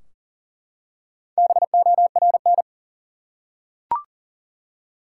Morse code, heard as a keyed tone, about 37 wpm, 700 Hz. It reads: BORN
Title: Four Letter Words - 40wpm